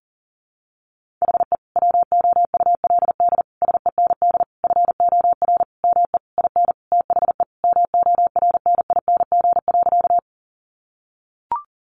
Morse code, heard as a keyed tone, about 40 wpm, 700 Hz. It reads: HE WOULD SEND FOR ME IN THE MORNING.
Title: Sentences from Top 1500 Words 40wpm